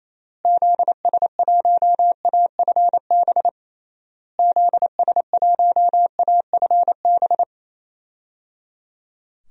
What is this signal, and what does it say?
Morse code, heard as a keyed tone, about 28 wpm, 700 Hz. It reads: ZS1AFB ZS1AFB
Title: DXCC-100 Call Signs - 6 Characters - Repeated 3x - 28wpm